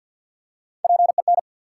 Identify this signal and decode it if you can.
Morse code, heard as a keyed tone, about 53 wpm, 700 Hz. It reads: PEN
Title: Three Letter Words - 50wpm